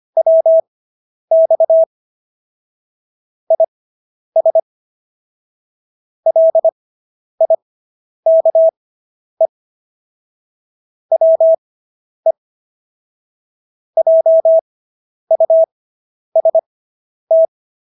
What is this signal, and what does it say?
Morse code, heard as a keyed tone, about 25 wpm, spaced as for 10 wpm, 650 Hz. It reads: WX IS LIKE WE JUST